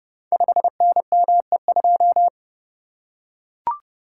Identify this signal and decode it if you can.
Morse code, heard as a keyed tone, about 30 wpm, 700 Hz. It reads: 5NME2
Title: Code Groups - 5 Letter:Numbers - 30wpm